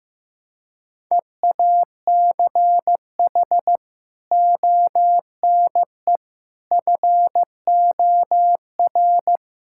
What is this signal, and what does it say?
Morse code, heard as a keyed tone, about 15 wpm, 700 Hz. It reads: EACH ONE FOR
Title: Sentences from Top 1600 Words 15wpm